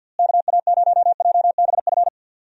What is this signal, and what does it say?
Morse code, heard as a keyed tone, about 52 wpm, 700 Hz. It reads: KA0JBF
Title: USA Call Signs - 6 Characters - Repeated 3x - 50wpm